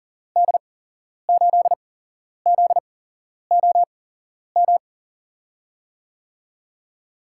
Morse code, heard as a keyed tone, about 40 wpm, spaced as for 11 wpm, 700 Hz. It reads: D8ZOM
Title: USA Call Signs - 6 Characters - 8x Character Spacing - 40wpm